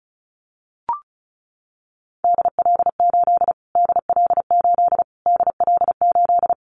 Morse code, heard as a keyed tone, about 34 wpm, 700 Hz. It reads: DL8 DL8 DL8
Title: Callsign Prefixes for Top 25 DXCC - Repeated 3x - 35wpm